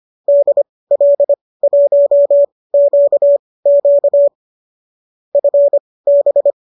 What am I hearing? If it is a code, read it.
Morse code, heard as a keyed tone, about 25 wpm, spaced as for 18 wpm, 570 Hz. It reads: DL1QQ FB